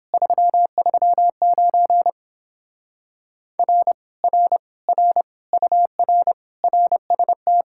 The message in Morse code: 339 R R R UR RST

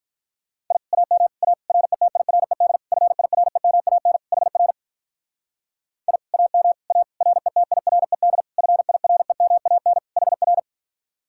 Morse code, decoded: I AM A RETIRED FIREMAN HR I AM A RETIRED FIREMAN HR

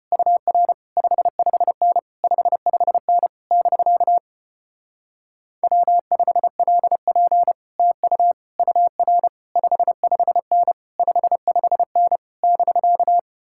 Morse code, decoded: UR 55N 55N <BK> W5LP TU UR 55N 55N <BK>

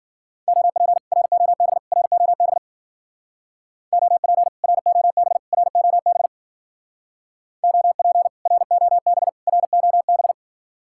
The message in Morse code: OP ROB ROB OP ROB ROB OP ROB ROB

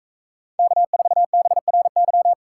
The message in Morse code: K4BRY

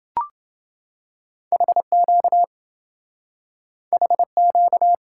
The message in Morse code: HQ HQ

